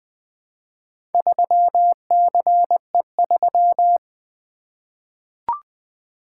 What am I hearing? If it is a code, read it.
Morse code, heard as a keyed tone, about 20 wpm, 700 Hz. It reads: 3CE3
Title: Four Letter-Number Combinations - 20wpm